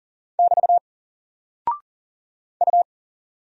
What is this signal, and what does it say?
Morse code, heard as a keyed tone, about 41 wpm, 700 Hz. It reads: <BT> U
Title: Single Character (Letter:Number:Punctuation:Prosign) Alt 40wpm